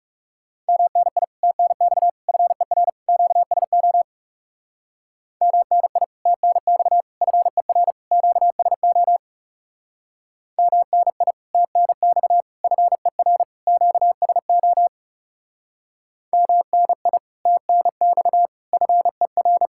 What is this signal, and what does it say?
Morse code, heard as a keyed tone, about 37 wpm, 700 Hz. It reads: MNI TNX FER QSO MNI TNX FER QSO MNI TNX FER QSO MNI TNX FER
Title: Speed-Racing Common QSO Phrases - 30wpm